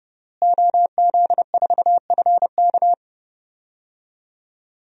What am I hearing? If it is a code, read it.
Morse code, heard as a keyed tone, about 32 wpm, 700 Hz. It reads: OZ4FK
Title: DXCC-100 Call Signs - 5 Characters - Repeated 2x - 30wpm